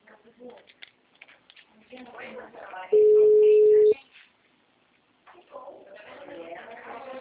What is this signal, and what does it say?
Ringback tone (Europe, 425 Hz), with speech.